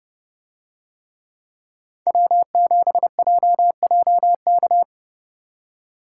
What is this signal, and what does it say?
Morse code, heard as a keyed tone, about 30 wpm, 700 Hz. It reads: W7JJK